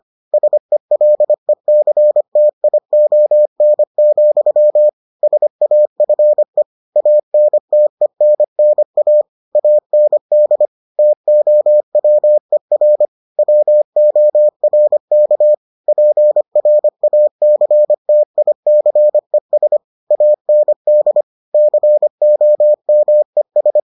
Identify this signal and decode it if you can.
Morse code, heard as a keyed tone, about 25 wpm, 600 Hz. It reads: SELECTION, SAFE ANTENNA AND TOWER WORK PRACTICES AND COMES